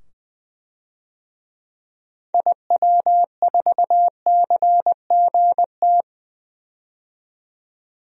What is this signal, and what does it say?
Morse code, heard as a keyed tone, about 20 wpm, 700 Hz. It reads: IW4CGT
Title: DXCC-100 Call Signs - 6 Characters - 20wpm